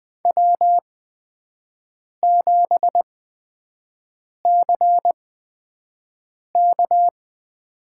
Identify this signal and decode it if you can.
Morse code, heard as a keyed tone, about 20 wpm, spaced as for 5 wpm, 700 Hz. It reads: W7CK
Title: USA Call Signs - 5 Characters - 8x Character Spacing - 20wpm